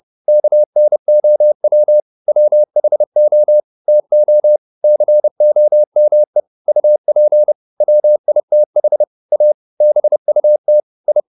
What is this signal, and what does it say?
Morse code, heard as a keyed tone, about 30 wpm, 600 Hz. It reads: KNOW WHO TO COME UP WITH A BUT I